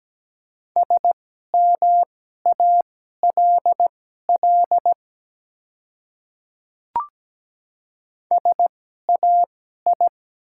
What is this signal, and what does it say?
Morse code, heard as a keyed tone, about 17 wpm, spaced as for 12 wpm, 700 Hz. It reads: SMALL SAI
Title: 2x Character Spacing - Top 200 Words - 17wpm